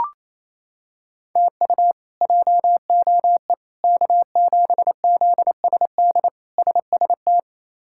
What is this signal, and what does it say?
Morse code, heard as a keyed tone, about 28 wpm, 700 Hz. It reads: TU JOE K7ZSD SST
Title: K1USN SST QSO Practice - 28wpm